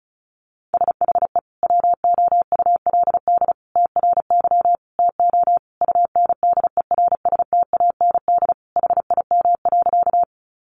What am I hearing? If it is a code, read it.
Morse code, heard as a keyed tone, about 34 wpm, 700 Hz. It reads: SHE WOULD TRY TO UNDERSTAND HIM.
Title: Sentences from Top 600 Words 35wpm